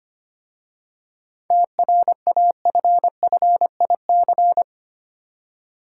Morse code, heard as a keyed tone, about 25 wpm, 700 Hz. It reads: TRAFFIC
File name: practice / 25wpm / Easy Words with Hints - 25wpm